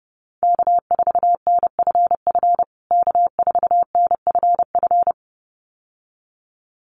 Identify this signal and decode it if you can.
Morse code, heard as a keyed tone, about 30 wpm, 700 Hz. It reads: K4NFF K4NFF
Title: USA Call Signs Repeated 2x 30wpm